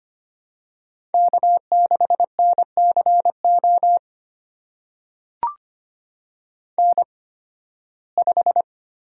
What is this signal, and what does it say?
Morse code, heard as a keyed tone, about 25 wpm, spaced as for 16 wpm, 700 Hz. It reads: K6NCO N5
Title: USA Call Signs - 5 Characters - 8x Character Spacing - 25wpm